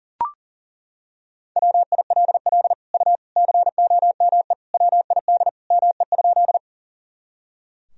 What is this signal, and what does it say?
Morse code, heard as a keyed tone, about 40 wpm, 700 Hz. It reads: WILL U COME WID ME?
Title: Sentences from Top 200 Words and Abbreviations 40wpm